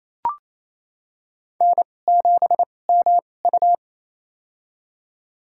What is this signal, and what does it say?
Morse code, heard as a keyed tone, about 27 wpm, spaced as for 20 wpm, 700 Hz. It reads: N7MU
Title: USA Call Signs - 4 Characters - 2x Character Spacing - 28wpm